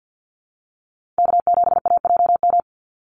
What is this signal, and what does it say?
Morse code, heard as a keyed tone, about 50 wpm, 700 Hz. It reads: K7AJM
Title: Call Signs - 50wpm